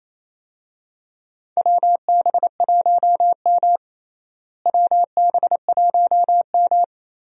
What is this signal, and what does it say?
Morse code, heard as a keyed tone, about 28 wpm, 700 Hz. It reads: WB1M WB1M
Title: USA Call Signs - 4 Characters - Repeated 2x - 28wpm